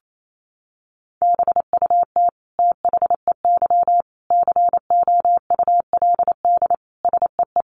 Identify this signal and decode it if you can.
Morse code, heard as a keyed tone, about 28 wpm, 700 Hz. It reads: BUT THEY COULD SEE
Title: Sentences from Top 500 Words 28wpm